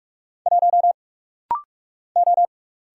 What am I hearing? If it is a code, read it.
Morse code, heard as a keyed tone, about 48 wpm, 700 Hz. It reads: 1 O
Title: Single Letter-Number - Rapid-Fire 45wpm